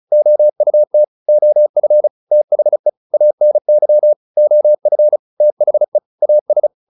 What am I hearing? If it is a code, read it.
Morse code, heard as a keyed tone, about 35 wpm, 600 Hz. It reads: OUT OF THE ANY OF THE AS